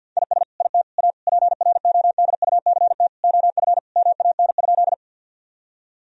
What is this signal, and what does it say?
Morse code, heard as a keyed tone, about 50 wpm, 700 Hz. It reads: IS IT A PRODUCT OF MAN?
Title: Sentences from Top 1000 Words - Review Part 2 - 50wpm